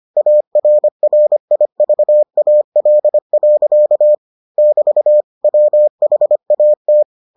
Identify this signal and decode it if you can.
Morse code, heard as a keyed tone, about 25 wpm, 600 Hz. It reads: ARRIVAL. <BT> WHAT